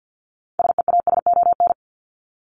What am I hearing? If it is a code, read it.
Morse code, heard as a keyed tone, about 50 wpm, 700 Hz. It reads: SEASON
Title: Top 1251-1500 Words 50wpm